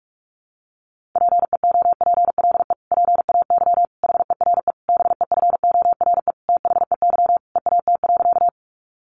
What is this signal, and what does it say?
Morse code, heard as a keyed tone, about 44 wpm, 700 Hz. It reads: PEOPLE PAY HERE BEFORE THEY EAT.
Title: Sentences from Top 700 Words 45wpm